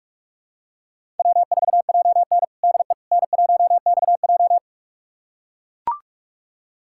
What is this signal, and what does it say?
Morse code, heard as a keyed tone, about 42 wpm, 700 Hz. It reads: W4JN DE N1XJ
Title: USA Callsign DE Callsign - 4 Characters - 45wpm